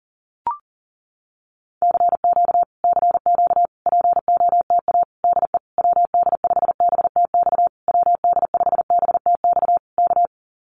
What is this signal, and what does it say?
Morse code, heard as a keyed tone, about 40 wpm, 700 Hz. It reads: CQ CQ POTA DE WD5BTX WD5BTX K
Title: Common QSO Phrases Encore 40wpm